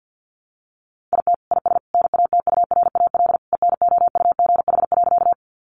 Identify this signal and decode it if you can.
Morse code, heard as a keyed tone, about 49 wpm, 700 Hz. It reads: IT IS NATURAL ENOUGH.
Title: Sentences from Top 700 Words 50wpm